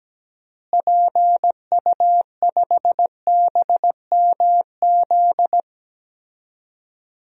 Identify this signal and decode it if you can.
Morse code, heard as a keyed tone, about 17 wpm, 700 Hz. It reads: PU5BMZ
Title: DXCC-100 Call Signs - 6 Characters - Repeated 3x - 17wpm